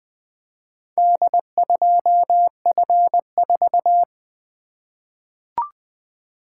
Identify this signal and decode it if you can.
Morse code, heard as a keyed tone, about 20 wpm, 700 Hz. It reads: D2F4
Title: Four Letter-Number Combinations - 20wpm